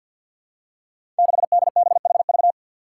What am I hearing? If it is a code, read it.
Morse code, heard as a keyed tone, about 52 wpm, 700 Hz. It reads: 6DBHV
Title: Five Challenging Letter-Number Combinations 50wpm